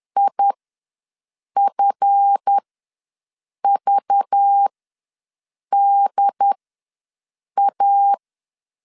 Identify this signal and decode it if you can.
Morse code, heard as a keyed tone, about 10 wpm, spaced as for 6 wpm, 800 Hz. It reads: IFVDA
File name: LEZIONI / 7 GF9Z / 7 GF9 ws 10wpm 4034